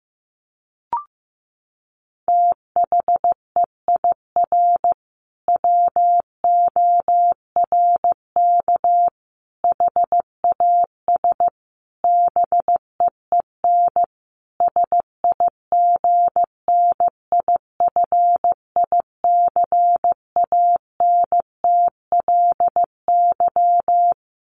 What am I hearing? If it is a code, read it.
Morse code, heard as a keyed tone, about 15 wpm, 700 Hz. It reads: THEIR WORK HAS BEEN SIGNIFICANTLY
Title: Sentences from Top 1500 Words 15wpm